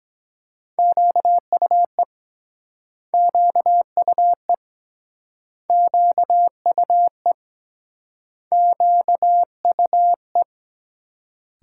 Morse code, heard as a keyed tone, about 23 wpm, 700 Hz. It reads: QUE QUE QUE QUE